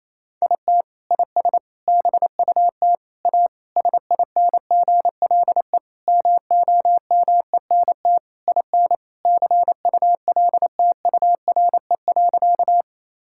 Morse code, heard as a keyed tone, about 27 wpm, 700 Hz. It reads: IT IS BUT A SINGLE MOMENT IN CULTURE.
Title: Sentences from Top 1000 Words - Review Part 6 - 28wpm